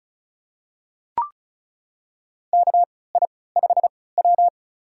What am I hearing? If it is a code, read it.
Morse code, heard as a keyed tone, about 36 wpm, spaced as for 20 wpm, 700 Hz. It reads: KI5W
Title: USA Call Signs - 4 Characters - 3x Character Spacing - 35wpm